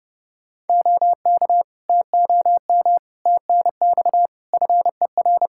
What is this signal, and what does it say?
Morse code, heard as a keyed tone, about 29 wpm, 700 Hz. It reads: OK TOM TNX FER
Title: Practice QSO - Exchange 4 - Ending - 30wpm